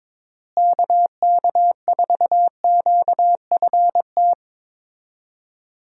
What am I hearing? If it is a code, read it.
Morse code, heard as a keyed tone, about 22 wpm, 700 Hz. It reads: KK4QFT
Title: USA Call Signs - 6 Characters - Repeated 3x - 22wpm